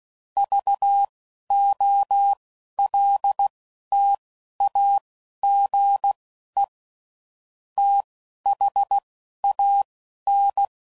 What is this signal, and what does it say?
Morse code, heard as a keyed tone, about 16 wpm, spaced as for 12 wpm, 800 Hz. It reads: VOLTAGE THAN
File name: CW Code Groups @10WPM / Technical Subjects / SwPwrSup